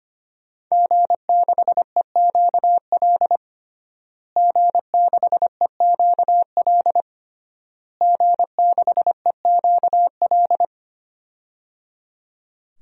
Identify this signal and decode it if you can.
Morse code, heard as a keyed tone, about 25 wpm, 700 Hz. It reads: G6EQL G6EQL G6EQL
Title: DXCC-100 Call Signs - 5 Characters - Repeated 3x - 25wpm